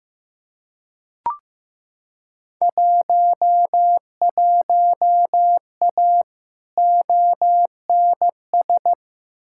Morse code, heard as a keyed tone, about 15 wpm, 700 Hz. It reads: 11A ONS
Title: Field Day Exchanges Encore - 15wpm